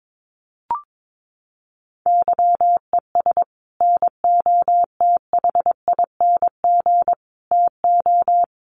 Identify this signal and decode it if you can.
Morse code, heard as a keyed tone, about 22 wpm, 700 Hz. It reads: YES NOTHING TO
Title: Sentences from Top 1300 Words 22wpm